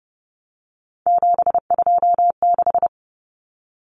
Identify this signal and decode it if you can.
Morse code, heard as a keyed tone, about 30 wpm, 700 Hz. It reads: 726
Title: Numbers - 3-Digit 30wpm